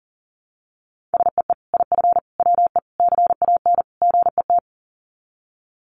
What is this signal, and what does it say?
Morse code, heard as a keyed tone, about 37 wpm, 700 Hz. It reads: SEE IF WE CAN GET
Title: Sets of 5 Words Encore2 40wpm